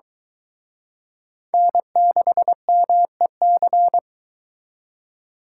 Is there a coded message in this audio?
Yes: Morse code from 1.5 to 4.0 s, heard as a keyed tone, about 23 wpm, 700 Hz. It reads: N6MEC